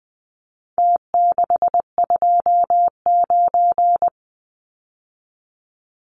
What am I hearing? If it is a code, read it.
Morse code, heard as a keyed tone, about 20 wpm, 700 Hz. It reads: T629
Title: Numbers - Contest Style - 4-Digit 20wpm